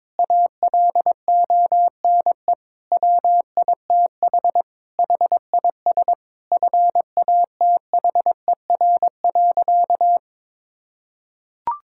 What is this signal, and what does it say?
Morse code, heard as a keyed tone, about 22 wpm, 700 Hz. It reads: ALONE WITH HIS FATHER.
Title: Sentences from Top 1000 Words - Review Part 5 - 22wpm